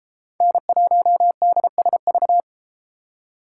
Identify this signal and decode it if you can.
Morse code, heard as a keyed tone, about 32 wpm, 700 Hz. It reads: N1DSV